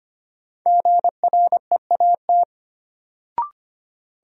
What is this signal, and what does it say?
Morse code, heard as a keyed tone, about 25 wpm, 700 Hz. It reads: GREAT
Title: Sentences from Top 200 Words - New Word Drill - 25wpm